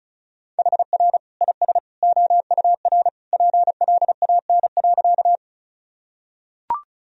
Morse code, heard as a keyed tone, about 36 wpm, 700 Hz. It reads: HR IS OUR PLAN.
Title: Sentences from Top 500 Words and Abbreviations 35wpm